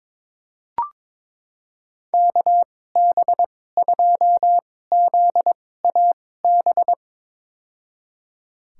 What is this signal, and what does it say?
Morse code, heard as a keyed tone, about 22 wpm, spaced as for 16 wpm, 700 Hz. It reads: KB2ZAB
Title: USA Call Signs - 2x Character Spacing - 22wpm